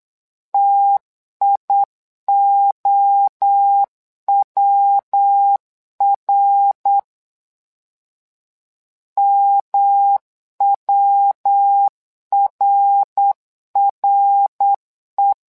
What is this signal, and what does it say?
Morse code, heard as a keyed tone, about 8 wpm, 800 Hz. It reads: TIOWR MWRRE
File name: LEZIONI / 3 MWRO / 3 MWRO 8wpm 2606